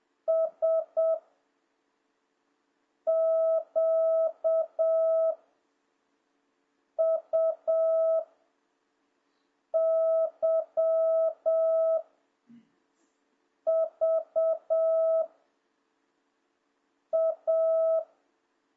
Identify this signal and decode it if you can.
Morse code, heard as a keyed tone, about 7 wpm, 650 Hz. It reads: S Q U Y V A